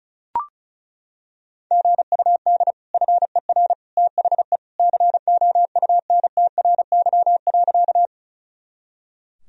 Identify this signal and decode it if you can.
Morse code, heard as a keyed tone, about 36 wpm, 700 Hz. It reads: GUD FER THE COUNTRY.
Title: Sentences from Top 300 Words and Abbreviations 35wpm